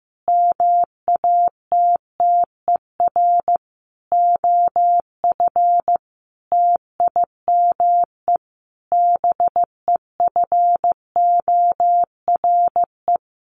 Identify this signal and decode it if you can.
Morse code, heard as a keyed tone, about 15 wpm, 700 Hz. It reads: MATTER OF TIME BEFORE
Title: Sets of 5 Words Encore 15wpm